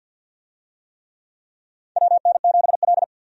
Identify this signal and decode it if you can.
Morse code, heard as a keyed tone, about 53 wpm, 700 Hz. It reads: WN7L
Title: USA Call Signs - 4 Characters - 50wpm